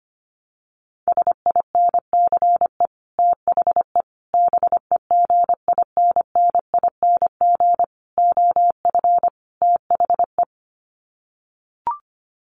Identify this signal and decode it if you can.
Morse code, heard as a keyed tone, about 25 wpm, 700 Hz. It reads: SINCE THE BEGINNING OF THE
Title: Sets of 5 Words Encore2 25wpm